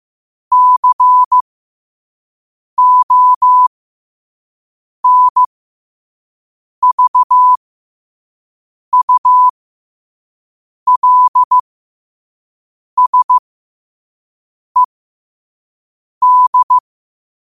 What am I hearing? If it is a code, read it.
Morse code, heard as a keyed tone, about 15 wpm, spaced as for 5 wpm, 1000 Hz. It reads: CONVULSED
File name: qotdfiles / 5wpm